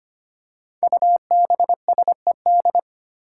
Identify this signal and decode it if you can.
Morse code, heard as a keyed tone, about 25 wpm, 700 Hz. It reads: UBSED
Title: Five Challenging Letter-Number Combinations 25wpm